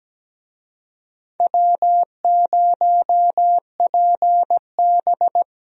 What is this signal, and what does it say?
Morse code, heard as a keyed tone, about 17 wpm, 700 Hz. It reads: W0PB
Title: USA Call Signs - 4 Characters - 17wpm